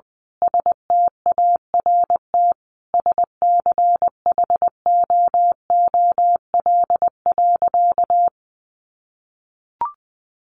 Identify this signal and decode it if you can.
Morse code, heard as a keyed tone, about 20 wpm, 700 Hz. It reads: START SCHOOL.
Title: Sentences from Top 1100 Words 20wpm